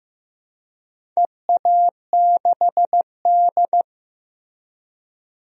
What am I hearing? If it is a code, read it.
Morse code, heard as a keyed tone, about 15 wpm, 700 Hz. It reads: EA6D
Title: DXCC-100 Call Signs - 4 Characters - Repeated 3x - 15wpm